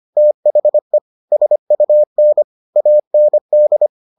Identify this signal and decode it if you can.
Morse code, heard as a keyed tone, about 25 wpm, 600 Hz. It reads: THE SUN AND